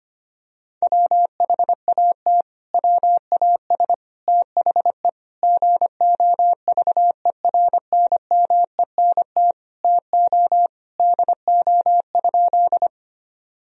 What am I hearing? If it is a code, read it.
Morse code, heard as a keyed tone, about 25 wpm, 700 Hz. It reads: WHAT WAS THE GOVERNMENT TO DO?
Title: Sentences from Top 1000 Words - Review Part 3 - 25wpm